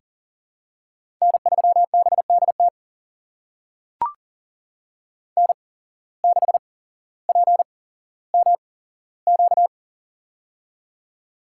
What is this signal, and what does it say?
Morse code, heard as a keyed tone, about 40 wpm, spaced as for 17 wpm, 700 Hz. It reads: N3BDT N6PMQ
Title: USA Call Signs - 5 Characters - 8x Character Spacing - 40wpm